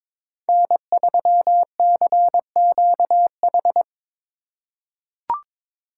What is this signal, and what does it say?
Morse code, heard as a keyed tone, about 22 wpm, 700 Hz. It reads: N3CQH
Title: USA Call Signs - 5 Characters - 3x Character Spacing - 22wpm